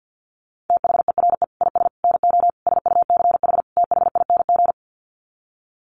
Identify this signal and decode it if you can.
Morse code, heard as a keyed tone, about 50 wpm, 700 Hz. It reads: THERE IS NO SUCH THING
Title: Sets of 5 Words Encore 50wpm